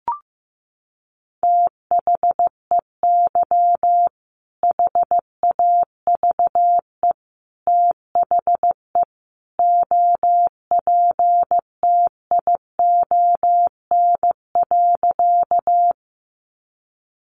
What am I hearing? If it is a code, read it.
Morse code, heard as a keyed tone, about 15 wpm, 700 Hz. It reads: THEY HAVE THE OPTION.
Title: Sentences from Top 1300 Words 15wpm